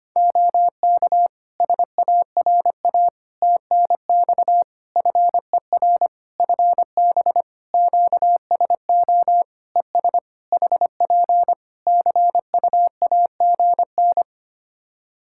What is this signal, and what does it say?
Morse code, heard as a keyed tone, about 25 wpm, 700 Hz. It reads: OK SARA TNX FER FB QSO ES HP CUAGN
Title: Practice QSO - Exchange 4 - Ending - 25wpm